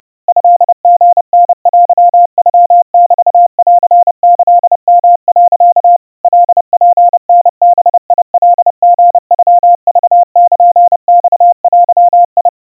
Morse code (Japanese Wabun code), heard as a keyed tone, about 30 wpm, 700 Hz. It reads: トリタテノメンキヨ、 カツタバカリノクルマデ
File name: 110 cpm 700 Hz Crispy